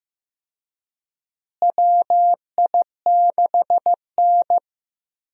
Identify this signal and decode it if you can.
Morse code, heard as a keyed tone, about 15 wpm, 700 Hz. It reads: WI6N
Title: USA Call Signs - 4 Characters - Repeated 3x - 15wpm